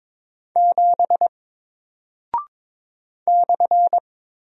Morse code, heard as a keyed tone, about 22 wpm, 700 Hz. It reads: 7 /
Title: Single Character (Letter:Number:Punctuation:Prosign) Alt 22wpm